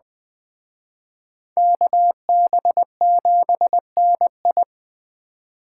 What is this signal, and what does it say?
Morse code, heard as a keyed tone, about 21 wpm, 700 Hz. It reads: KB7NI
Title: Speed Racing - USA 2x2 Call Signs 17wpm